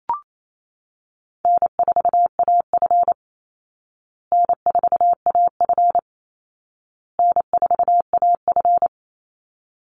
Morse code, heard as a keyed tone, about 28 wpm, 700 Hz. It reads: N4AF N4AF N4AF
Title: Call Signs Repeated 2x - 28wpm